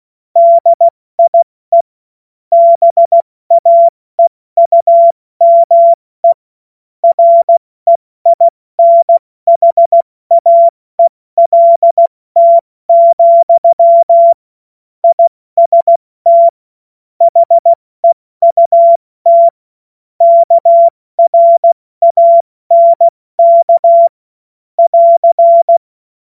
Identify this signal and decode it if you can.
Morse code, heard as a keyed tone, about 16 wpm, 680 Hz. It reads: DIE BAEUME REINHAELT, IST HEUT KRANK <AR>
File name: Morsekurs / Witze / Witz 22. August 2023 70 F80